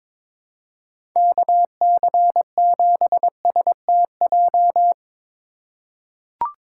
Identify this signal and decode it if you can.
Morse code, heard as a keyed tone, about 22 wpm, 700 Hz. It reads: KC7STJ
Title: USA Call Signs - 6 Characters - 3x Character Spacing - 22wpm